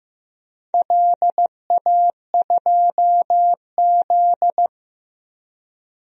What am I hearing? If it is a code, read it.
Morse code, heard as a keyed tone, about 16 wpm, 700 Hz. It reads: LA2Z
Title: DXCC-100 Call Signs - 4 Characters - 15wpm